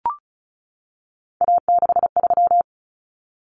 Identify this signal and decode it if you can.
Morse code, heard as a keyed tone, about 34 wpm, 700 Hz. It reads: A63
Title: Common Cut Numbers - 3-Digit 35wpm